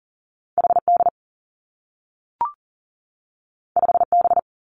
Morse code, heard as a keyed tone, about 42 wpm, 700 Hz. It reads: H D 5 B
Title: Two Challenging Letter-Number Combinations 40wpm